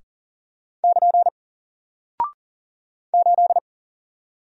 Morse code, heard as a keyed tone, about 40 wpm, 700 Hz. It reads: <KN> 8
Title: Single Character (Letter:Number:Punctuation:Prosign) Alt 40wpm